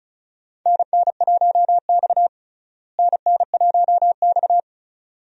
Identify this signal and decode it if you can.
Morse code, heard as a keyed tone, about 36 wpm, 700 Hz. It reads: NN1X NN1X
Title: USA Call Signs - 4 Characters - Repeated 2x - 35wpm